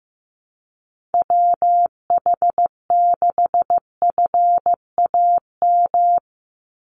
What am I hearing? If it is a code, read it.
Morse code, heard as a keyed tone, about 15 wpm, 700 Hz. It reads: WH6FAM
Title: USA Call Signs 15wpm